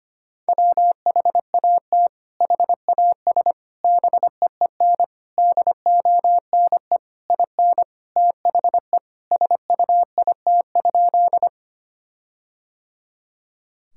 Morse code, heard as a keyed tone, about 25 wpm, 700 Hz. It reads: WHAT HAS BEEN DONE IN THE SUIT?
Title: Sentences from Top 1600 Words 25wpm